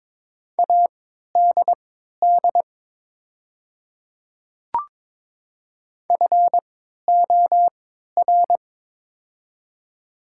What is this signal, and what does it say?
Morse code, heard as a keyed tone, about 22 wpm, spaced as for 13 wpm, 700 Hz. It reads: ADD FOR
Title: 3x Character Spacing - Top 300 Words - 22wpm